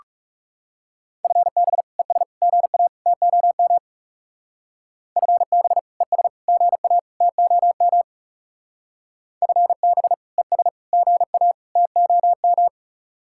Morse code, heard as a keyed tone, about 38 wpm, 700 Hz. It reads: FB ES GA TOM FB ES GA TOM FB ES GA TOM